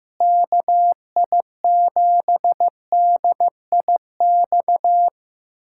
Morse code, heard as a keyed tone, about 15 wpm, 700 Hz. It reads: KI7DIX
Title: USA Callsign DE Callsign - 6 Characters - 15wpm